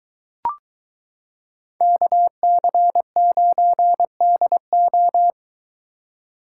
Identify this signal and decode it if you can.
Morse code, heard as a keyed tone, about 23 wpm, 700 Hz. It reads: KC9DO